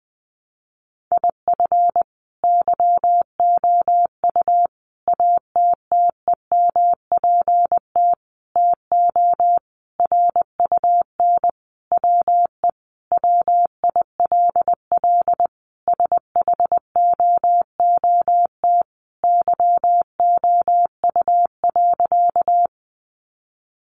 Morse code, heard as a keyed tone, about 20 wpm, 700 Hz. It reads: IF YOU ATTEMPT TO RUN WE WILL SHOOT YOU.
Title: Sentences from Top 1200 Words 20wpm